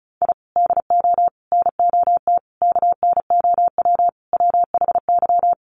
Morse code, heard as a keyed tone, about 34 wpm, 700 Hz. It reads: I DO NOT KNOW WHY
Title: Sets of 5 Words 35wpm